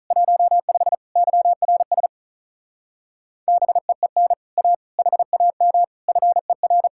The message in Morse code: 15 YRS BEEN A HAM FER

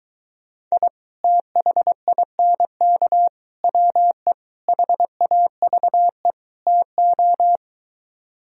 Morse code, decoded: I THINK WE HAVE TO

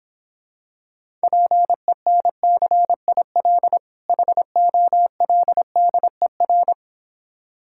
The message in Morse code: PENCIL HOLDER

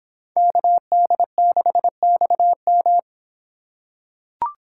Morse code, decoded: KD6XM

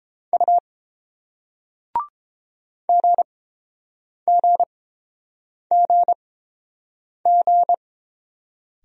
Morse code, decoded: U GGGG

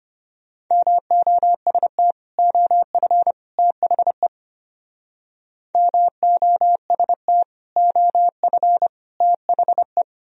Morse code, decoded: MOST OF THE MOST OF THE

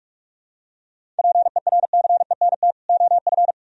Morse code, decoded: PERCENT OF

